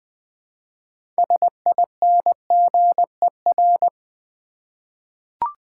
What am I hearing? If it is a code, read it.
Morse code, heard as a keyed tone, about 20 wpm, 700 Hz. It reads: SINGER
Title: Six Letter Words - 20wpm